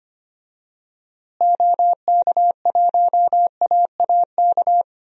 Morse code, heard as a keyed tone, about 25 wpm, 700 Hz. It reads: OK1AAK
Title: DXCC-100 Call Signs - 6 Characters - Repeated 2x - 25wpm